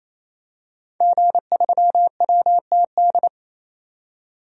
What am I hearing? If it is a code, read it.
Morse code, heard as a keyed tone, about 28 wpm, 700 Hz. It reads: G3WTD
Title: DXCC-100 Call Signs - 5 Characters - 28wpm